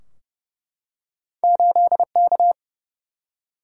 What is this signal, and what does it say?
Morse code, heard as a keyed tone, about 30 wpm, 700 Hz. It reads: 8K
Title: Single-Letter Single-Number 30wpm